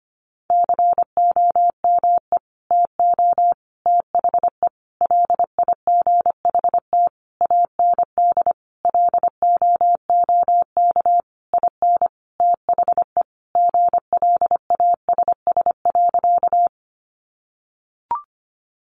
Morse code, heard as a keyed tone, about 25 wpm, 700 Hz. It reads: COME TO THE LIGHT AND LOOK IN THE GLASS.
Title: Sentences from Top 1300 Words 25wpm